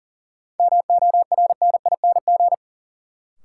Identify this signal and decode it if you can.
Morse code, heard as a keyed tone, about 40 wpm, 700 Hz. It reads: MORNING
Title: Sets of 3 Words - New Word Drill - 40wpm